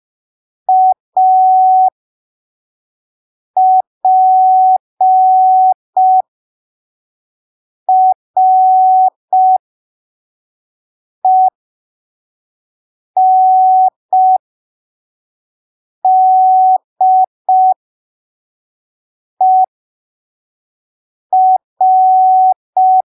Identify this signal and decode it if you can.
Morse code, heard as a keyed tone, about 5 wpm, spaced as for 3 wpm, 750 Hz. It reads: APRENDER
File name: arquivos / texto / texto 21 05ppm